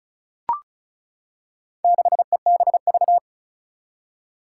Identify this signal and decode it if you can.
Morse code, heard as a keyed tone, about 36 wpm, 690 Hz. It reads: 6EBV
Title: Four Challenging Letter-Number Combinations 35wpm